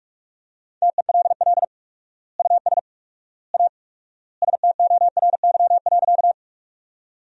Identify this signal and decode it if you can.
Morse code, heard as a keyed tone, about 41 wpm, 700 Hz. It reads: TELL US A STORY.
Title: 4x Word-Spacing - Sentences from Top 300 Words - 45wpm